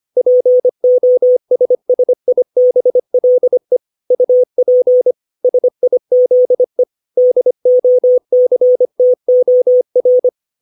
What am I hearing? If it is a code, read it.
Morse code, heard as a keyed tone, about 25 wpm, 500 Hz. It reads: POSSIBLE UP SIZE DOCTOR
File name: 632 words / 632 Words 25WPM